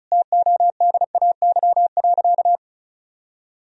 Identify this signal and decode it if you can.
Morse code, heard as a keyed tone, about 36 wpm, 700 Hz. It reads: TODAY.
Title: Sentences from Top 1000 Words - Review Part 2 - 35wpm